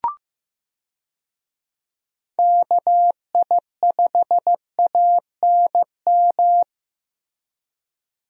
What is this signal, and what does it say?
Morse code, heard as a keyed tone, about 15 wpm, 700 Hz. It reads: KI5ANM
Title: USA Call Signs - 6 Characters - Repeated 3x - 15wpm